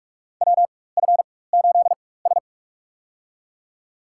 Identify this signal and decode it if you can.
Morse code, heard as a keyed tone, about 46 wpm, spaced as for 21 wpm, 700 Hz. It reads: WF8S
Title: USA Call Signs - 4 Characters - 4x Character Spacing - 45wpm